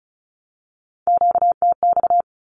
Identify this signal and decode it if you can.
Morse code, heard as a keyed tone, about 34 wpm, 700 Hz. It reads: QTX
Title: Q-Codes 35wpm